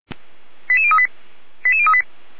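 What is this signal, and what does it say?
Five-tone selective call (ZVEI), heard as quick runs of five tones, tones of about 70 ms. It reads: 88038 88038
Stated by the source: Five-tone ZVEI selective call 88149, received on SDR.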